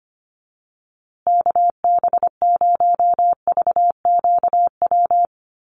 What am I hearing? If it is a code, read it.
Morse code, heard as a keyed tone, about 25 wpm, 700 Hz. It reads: KB0VQW
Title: USA Call Signs Repeated 2x 25wpm